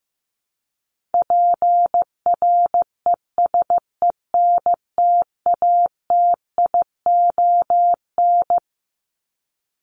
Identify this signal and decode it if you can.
Morse code, heard as a keyed tone, about 15 wpm, 700 Hz. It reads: PRESENTATION
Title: Top 1501-1750 Words 15wpm